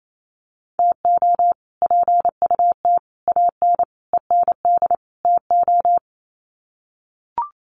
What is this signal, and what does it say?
Morse code, heard as a keyed tone, about 28 wpm, 700 Hz. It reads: TO PUT AN END TO
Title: Sets of 5 Words Encore2 28wpm